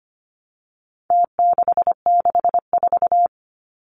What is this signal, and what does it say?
Morse code, heard as a keyed tone, about 25 wpm, 700 Hz. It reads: T664
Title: Numbers - Contest Style - 4-Digit 25wpm